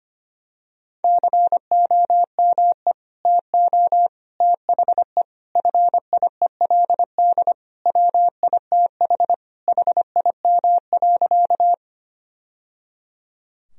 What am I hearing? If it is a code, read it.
Morse code, heard as a keyed tone, about 25 wpm, 700 Hz. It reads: COME TO THE FIELD WITH HIM.
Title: Sentences from Top 1000 Words - Review Part 2 - 25wpm